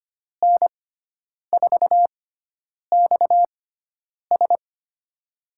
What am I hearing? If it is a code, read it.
Morse code, heard as a keyed tone, about 25 wpm, spaced as for 9 wpm, 700 Hz. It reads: N4XS
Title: USA Call Signs - 6x Character Spacing - 25wpm